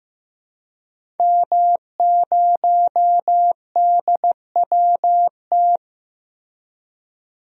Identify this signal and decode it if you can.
Morse code, heard as a keyed tone, about 15 wpm, 700 Hz. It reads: M0DWT
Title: DXCC-100 Call Signs - 5 Characters - Repeated 3x - 15wpm